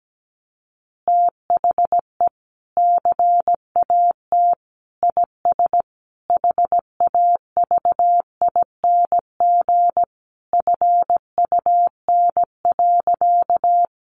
Morse code, heard as a keyed tone, about 17 wpm, 700 Hz. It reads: THE CAT IS HAVING FUN.
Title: Easy Phrases 17wpm